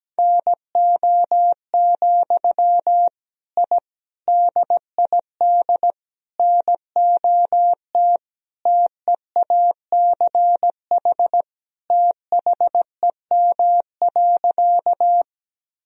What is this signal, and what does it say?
Morse code, heard as a keyed tone, about 17 wpm, 700 Hz. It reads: NO, I DID NOT TEACH THEM.
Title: Sentences from Top 1000 Words - Review Part 2 - 17wpm